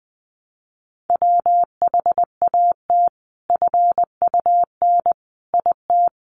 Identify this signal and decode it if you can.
Morse code, heard as a keyed tone, about 20 wpm, 700 Hz. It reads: WHAT FUN IT
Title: Sentences from Top 700 Words 20wpm